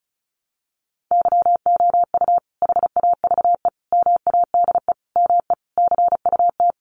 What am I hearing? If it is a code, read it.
Morse code, heard as a keyed tone, about 34 wpm, 700 Hz. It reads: YOU HAVE MADE ME CUT
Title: Sentences from Top 1300 Words 35wpm